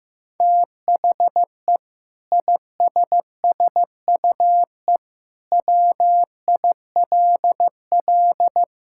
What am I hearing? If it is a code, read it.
Morse code, heard as a keyed tone, about 15 wpm, 700 Hz. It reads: THE ISSUE WILL
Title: Sentences from Top 1000 Words - Review Part 6 - 15wpm